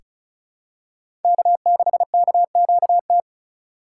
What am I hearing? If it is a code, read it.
Morse code, heard as a keyed tone, about 36 wpm, 700 Hz. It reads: K6KQT
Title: USA Call Signs - 5 Characters - 4x Character Spacing - 35wpm